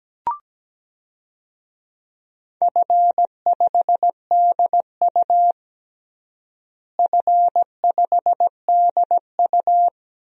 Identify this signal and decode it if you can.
Morse code, heard as a keyed tone, about 17 wpm, 700 Hz. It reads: F5DU F5DU
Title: DXCC-100 Call Signs - 4 Characters - Repeated 2x - 17wpm